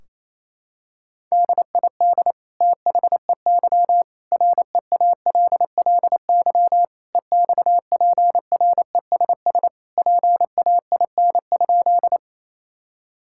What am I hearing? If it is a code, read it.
Morse code, heard as a keyed tone, about 27 wpm, 700 Hz. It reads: DID THEY REALLY EXPRESS PAIN?
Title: Sentences from Top 1600 Words 28wpm